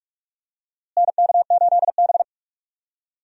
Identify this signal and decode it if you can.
Morse code, heard as a keyed tone, about 42 wpm, 700 Hz. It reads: NK8B
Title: USA Call Signs - 4 Characters - Repeated 2x - 45wpm